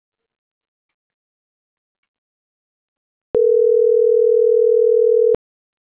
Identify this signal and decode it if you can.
Ringback tone (North America), 440 + 480 Hz.